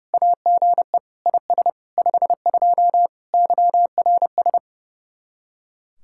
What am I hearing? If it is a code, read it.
Morse code, heard as a keyed tone, about 38 wpm, 700 Hz. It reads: AGE IS 52 YRS